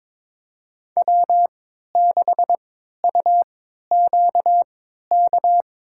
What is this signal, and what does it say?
Morse code, heard as a keyed tone, about 22 wpm, spaced as for 12 wpm, 700 Hz. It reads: W6UQK
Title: USA Call Signs - 5 Characters - 3x Character Spacing - 22wpm